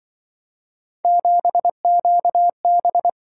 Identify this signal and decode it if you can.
Morse code, heard as a keyed tone, about 25 wpm, 700 Hz. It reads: 7QB
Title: Speed-Racing - Three Letter-Number Combinations - 20wpm